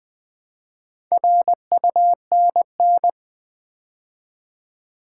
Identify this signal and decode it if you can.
Morse code, heard as a keyed tone, about 20 wpm, 700 Hz. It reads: RUNN
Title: Build A Word - 2 Letters At A Time - 6-Letter Words - 20wpm